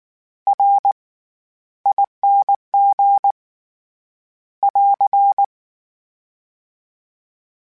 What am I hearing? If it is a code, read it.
Morse code, heard as a keyed tone, about 19 wpm, spaced as for 13 wpm, 800 Hz. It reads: RING <AR>
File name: Special INGs / Special INGs-mp3s / SlowToFaster ing @18